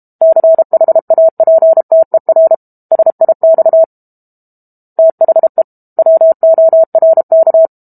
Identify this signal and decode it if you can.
Morse code, heard as a keyed tone, about 32 wpm, 650 Hz. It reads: CHAPTER SIX THE WORK
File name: Book Two / 06-Chapter 6 - 32 WPM